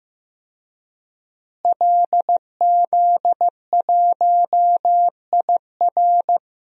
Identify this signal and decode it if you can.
Morse code, heard as a keyed tone, about 15 wpm, 700 Hz. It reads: LZ1IR
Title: DXCC-100 Call Signs - 5 Characters - Repeated 3x - 15wpm